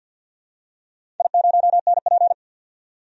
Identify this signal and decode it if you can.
Morse code, heard as a keyed tone, about 46 wpm, 700 Hz. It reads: I0NP
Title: DXCC-100 Call Signs - 4 Characters - 50wpm